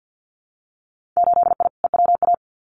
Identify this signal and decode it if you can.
Morse code, heard as a keyed tone, about 50 wpm, 700 Hz. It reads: 8I EWA
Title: Winter Field Day Exchanges Encore 50wpm